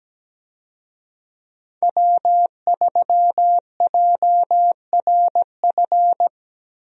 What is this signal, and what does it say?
Morse code, heard as a keyed tone, about 17 wpm, 700 Hz. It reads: W3JRF
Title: USA Call Signs - 5 Characters - Repeated 2x - 17wpm